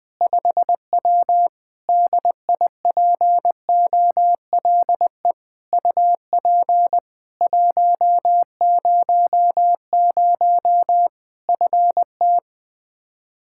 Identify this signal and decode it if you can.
Morse code, heard as a keyed tone, about 20 wpm, 700 Hz. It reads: 5W DIPOLE UP 100 FT
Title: K1USN SST QSO Practice 20wpm